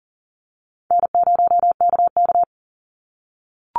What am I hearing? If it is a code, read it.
Morse code, heard as a keyed tone, about 40 wpm, 700 Hz. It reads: N0KK
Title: Easy Call Signs - 40wpm